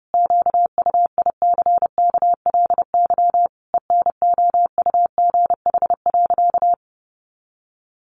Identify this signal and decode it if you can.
Morse code, heard as a keyed tone, about 30 wpm, 700 Hz. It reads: QUICKLY ENOUGH.
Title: Sentences from Top 700 Words 30wpm